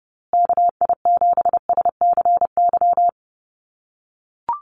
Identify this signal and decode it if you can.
Morse code, heard as a keyed tone, about 30 wpm, 700 Hz. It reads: KI7SCY
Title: USA Call Signs Repeated 3x 30wpm